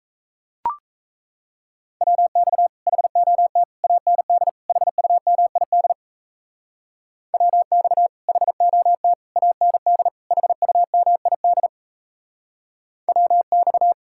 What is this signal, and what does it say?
Morse code, heard as a keyed tone, about 37 wpm, 700 Hz. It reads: WX HOT AND HUMID WX HOT AND HUMID WX